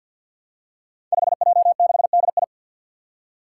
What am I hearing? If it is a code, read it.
Morse code, heard as a keyed tone, about 50 wpm, 700 Hz. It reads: 5J6DI
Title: Five Challenging Letter-Number Combinations 50wpm